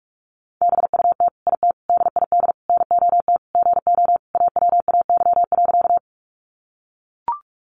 Morse code, heard as a keyed tone, about 44 wpm, 700 Hz. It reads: BUT IT DID NOT GO AWAY.
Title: Sentences from Top 600 Words 45wpm